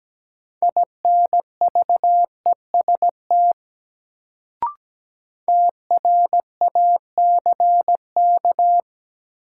Morse code, heard as a keyed tone, about 17 wpm, 700 Hz. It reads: INVEST TRACK
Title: Sentences from Top 1100 Words - New Word Drill - 17wpm